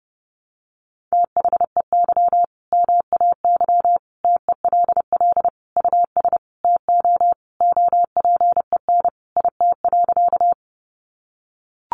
Morse code, heard as a keyed tone, about 30 wpm, 700 Hz. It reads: THEY MAY TELL US TO OPEN IT.
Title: Sentences from Top 300 Words 30wpm